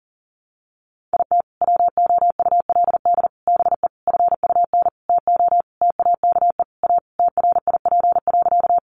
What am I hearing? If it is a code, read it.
Morse code, heard as a keyed tone, about 37 wpm, 700 Hz. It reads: IT WOULD BE FUN TO TAKE A TRIP.
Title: Sentences from Top 1300 Words 40wpm